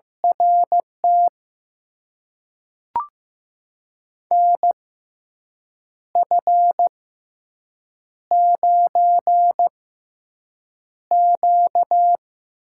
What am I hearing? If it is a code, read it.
Morse code, heard as a keyed tone, about 15 wpm, spaced as for 6 wpm, 700 Hz. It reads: RT NF9Q
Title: USA Call Signs - 4 Characters - 6x Character Spacing - 15wpm